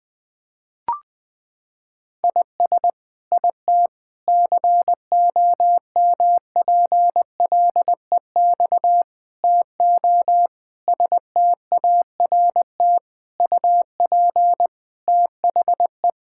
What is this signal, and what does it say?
Morse code, heard as a keyed tone, about 21 wpm, 700 Hz. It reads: IS IT COMPLEX TO START UP THE
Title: Sentences from Top 1000 Words - Review Part 6 - 20wpm